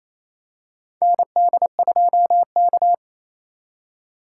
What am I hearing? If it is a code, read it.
Morse code, heard as a keyed tone, about 28 wpm, 700 Hz. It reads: ND2K
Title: USA Call Signs - 4 Characters - 2x Character Spacing - 28wpm